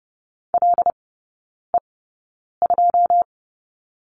Morse code, heard as a keyed tone, about 28 wpm, spaced as for 9 wpm, 690 Hz. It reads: LE2
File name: Lesson 16e - Letter c - Three Characters 30wpm